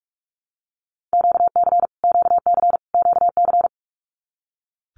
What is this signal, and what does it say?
Morse code, heard as a keyed tone, about 44 wpm, 700 Hz. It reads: QC QC QC
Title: Canadian Provinces and Territories - Repeated 3x - 45wpm